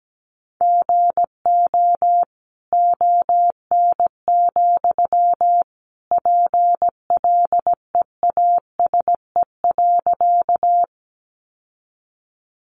Morse code, heard as a keyed tone, about 17 wpm, 700 Hz. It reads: GO ON, PLEASE.
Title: Sentences from Top 1500 Words 17wpm